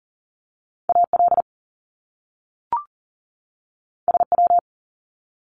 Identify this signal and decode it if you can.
Morse code, heard as a keyed tone, about 41 wpm, 700 Hz. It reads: AL SW
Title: Top Two Letter Combinations in Words 40wpm